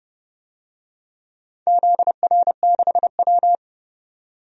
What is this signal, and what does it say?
Morse code, heard as a keyed tone, about 30 wpm, 700 Hz. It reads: ZR6W
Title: DXCC-100 Call Signs - 4 Characters - 30wpm